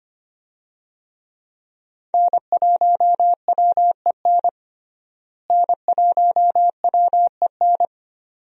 Morse code, heard as a keyed tone, about 25 wpm, 700 Hz. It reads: N1WEN N1WEN
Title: USA Call Signs - 5 Characters - Repeated 2x - 25wpm